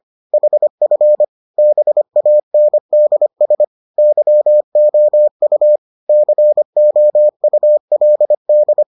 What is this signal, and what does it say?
Morse code, heard as a keyed tone, about 25 wpm, 600 Hz. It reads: HF BANDS YOU COULD